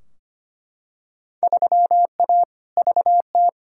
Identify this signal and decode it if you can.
Morse code, heard as a keyed tone, about 25 wpm, 700 Hz. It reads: 3A VT
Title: Field Day Exchanges Encore - 25wpm